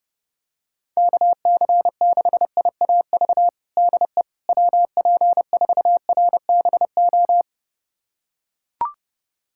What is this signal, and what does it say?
Morse code, heard as a keyed tone, about 32 wpm, 700 Hz. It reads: KC6IAV DE WP4RBO
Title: USA Callsign DE Callsign 30wpm